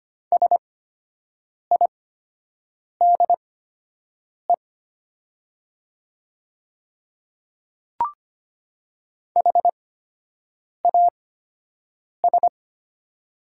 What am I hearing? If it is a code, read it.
Morse code, heard as a keyed tone, about 25 wpm, spaced as for 7 wpm, 700 Hz. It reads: SIDE HAS
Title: 8x Character Spacing - Top 300 Words - 25wpm